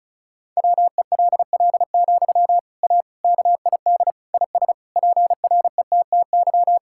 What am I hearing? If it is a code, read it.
Morse code, heard as a keyed tone, about 34 wpm, 700 Hz. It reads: WELL, A KID IS PRETTY
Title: Sentences from Top 1600 Words 35wpm